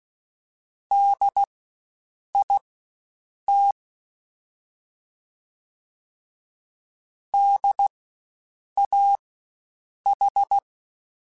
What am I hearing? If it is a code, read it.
Morse code, heard as a keyed tone, about 16 wpm, spaced as for 7 wpm, 800 Hz. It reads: DIT DAH
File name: Spaced Alphas / Spaced Alphas mp3s / Ds sp @15